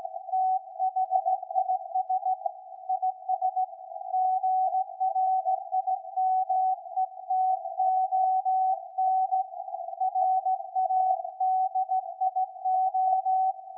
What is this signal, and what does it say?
Morse code, heard as a keyed tone, about 16 wpm, 750 Hz. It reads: THIS IS GRIMETON RADIO
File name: iz7slz / VLF / SAQ 13-02-2023